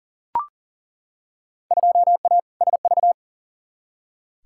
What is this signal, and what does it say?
Morse code, heard as a keyed tone, about 40 wpm, 700 Hz. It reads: 2A SV
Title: Field Day Exchanges - 40wpm